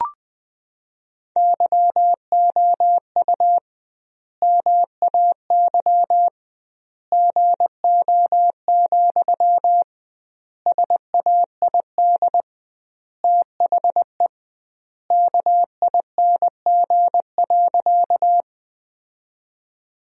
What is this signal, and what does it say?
Morse code, heard as a keyed tone, about 20 wpm, 700 Hz. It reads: YOU MAY GO, SAID THE KING.
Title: 2x Word-Spacing - Sentences from Top 400 Words - 20wpm